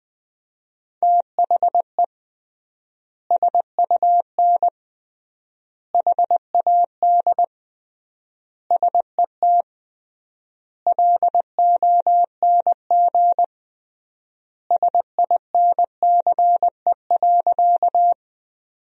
Morse code, heard as a keyed tone, about 20 wpm, 700 Hz. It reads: THE SUN HAD SET LONG SINCE.
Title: 3x Word-Spacing - Sentences from Top 300 Words - 20wpm